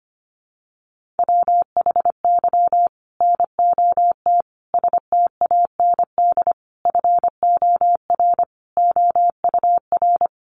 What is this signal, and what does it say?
Morse code, heard as a keyed tone, about 25 wpm, 700 Hz. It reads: WHY NOT STAND FOR OUR
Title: Sentences from Top 1500 Words 25wpm